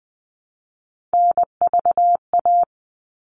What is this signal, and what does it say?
Morse code, heard as a keyed tone, about 20 wpm, 700 Hz. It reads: NVA
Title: All Cut Numbers - 3-Digit 20wpm